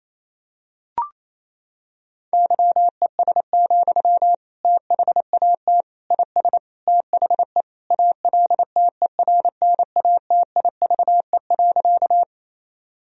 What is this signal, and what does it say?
Morse code, heard as a keyed tone, about 28 wpm, 700 Hz. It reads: YES, THAT IS THE ALTERNATIVE.
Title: Sentences from Top 1000 Words - Review Part 4 - 28wpm